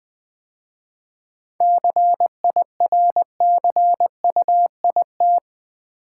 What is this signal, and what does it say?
Morse code, heard as a keyed tone, about 20 wpm, 700 Hz. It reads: CIRCUIT
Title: 7 Letter Words - First 3 Letters Given - 20wpm